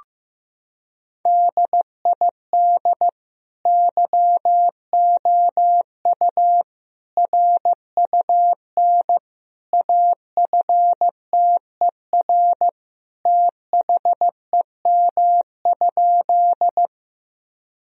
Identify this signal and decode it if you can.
Morse code, heard as a keyed tone, about 15 wpm, 700 Hz. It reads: DID YOU RUN AFTER THEM?
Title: Sentences from Top 1000 Words - Review Part 1 - 15wpm